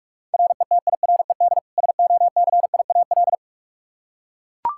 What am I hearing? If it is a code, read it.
Morse code, heard as a keyed tone, about 42 wpm, 700 Hz. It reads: RETIRED SOCIAL
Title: Practice QSO - Exchange 3 - More Info - 45wpm